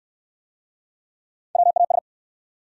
Morse code, heard as a keyed tone, about 63 wpm, 700 Hz. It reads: WAS